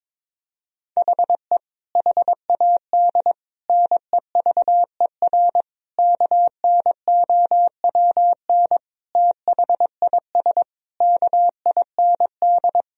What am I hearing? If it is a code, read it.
Morse code, heard as a keyed tone, about 22 wpm, 700 Hz. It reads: HE HAD NEVER KNOWN THIS KIND
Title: Sentences from Top 1000 Words - Review Part 3 - 22wpm